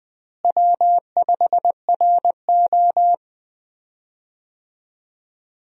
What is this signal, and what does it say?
Morse code, heard as a keyed tone, about 21 wpm, 700 Hz. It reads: W5RO
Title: USA Call Signs - 4 Characters - Repeated 2x - 20wpm